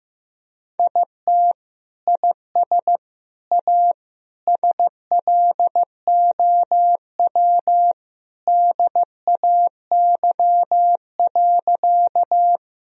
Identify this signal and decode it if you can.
Morse code, heard as a keyed tone, about 16 wpm, 700 Hz. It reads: IT IS A SLOW DAY.
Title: Sentences from Top 1000 Words - Review Part 2 - 15wpm